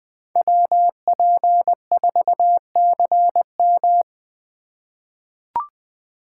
Morse code, heard as a keyed tone, about 20 wpm, 700 Hz. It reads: WP4CM
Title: USA Call Signs - 3x Character Spacing - 20wpm